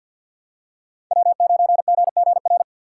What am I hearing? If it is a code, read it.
Morse code, heard as a keyed tone, about 49 wpm, 700 Hz. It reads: W9GGR
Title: USA Call Signs - 5 Characters - 4x Character Spacing - 50wpm